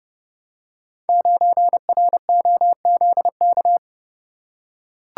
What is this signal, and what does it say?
Morse code, heard as a keyed tone, about 29 wpm, 700 Hz. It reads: 9ROZK
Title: Code Groups - 5 Letter:Numbers - 30wpm